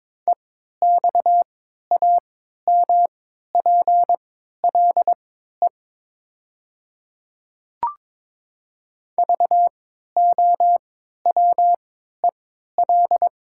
Morse code, heard as a keyed tone, about 22 wpm, spaced as for 12 wpm, 700 Hz. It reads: EXAMPLE VOWEL
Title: 3x Character Spacing - Top 500 Words - 22wpm